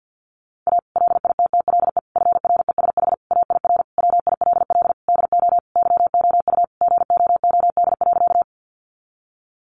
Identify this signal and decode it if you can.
Morse code, heard as a keyed tone, about 50 wpm, 700 Hz. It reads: A LITTLE FRESH AIR WILL DO YOU GOOD.
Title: Sentences from Top 1200 Words 50wpm